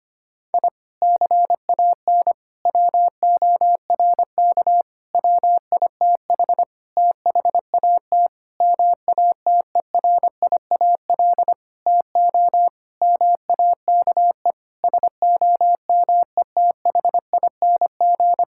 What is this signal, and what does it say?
Morse code, heard as a keyed tone, about 25 wpm, 700 Hz. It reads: I CAN WORK WITH THAT MATERIAL TO MAKE SOMETHING
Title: Sentences from Top 1000 Words - Review Part 5 - 25wpm